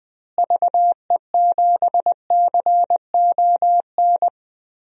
Morse code, heard as a keyed tone, about 20 wpm, 700 Hz. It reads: VE7CON
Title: DXCC-100 Call Signs - 6 Characters - Repeated 2x - 20wpm